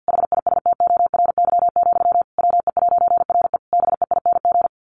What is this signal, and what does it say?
Morse code, heard as a keyed tone, about 50 wpm, 700 Hz. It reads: HISTORY, WE'RE BEING
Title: A Sound Of Thunder 50wpm